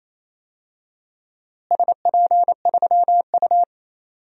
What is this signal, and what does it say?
Morse code, heard as a keyed tone, about 28 wpm, 700 Hz. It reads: SP3U
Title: DXCC-100 Call Signs - 4 Characters - Repeated 2x - 28wpm